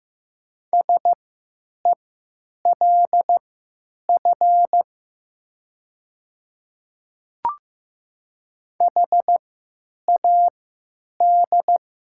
Morse code, heard as a keyed tone, about 15 wpm, spaced as for 9 wpm, 690 Hz. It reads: SELF HAD
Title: 3x Character Spacing - Top 300 Words - 15wpm